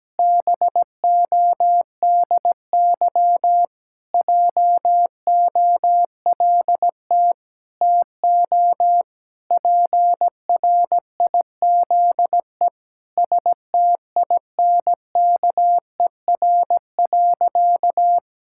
Morse code, heard as a keyed tone, about 17 wpm, 700 Hz. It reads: BODY JOLT TO PRIZE STINKER.
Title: Pangram Challenge v2 - 17wpm